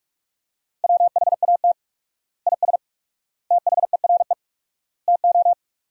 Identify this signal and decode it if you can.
Morse code, heard as a keyed tone, about 41 wpm, 700 Hz. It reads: WHAT IS THERE TO
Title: 4x Word-Spacing - Sentences from Top 100 Words - 45wpm